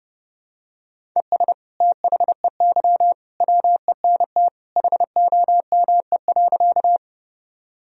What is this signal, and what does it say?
Morse code, heard as a keyed tone, about 30 wpm, 700 Hz. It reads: ES THEY WENT HOME.
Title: Sentences from Top 200 Words and Abbreviations 30wpm